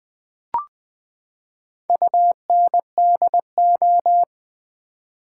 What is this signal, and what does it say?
Morse code, heard as a keyed tone, about 20 wpm, 700 Hz. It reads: UNDO
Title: Build A Word - 4 Letters At A Time - 11-Letter Words - 20wpm